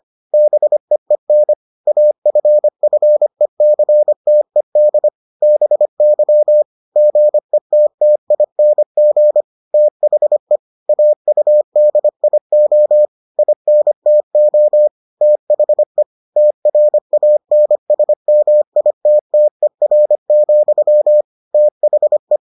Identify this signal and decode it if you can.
Morse code, heard as a keyed tone, about 25 wpm, 600 Hz. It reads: BEEN AFFECTED BY GETTING THE AUDIO INTO THE TRANSMITTER, THE